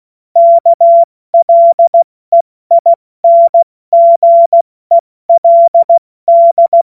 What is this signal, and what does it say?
Morse code, heard as a keyed tone, about 16 wpm, 680 Hz. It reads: KLEINGELD
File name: Morsekurs / Witze / Witz 11. Mai 2021, 70 F80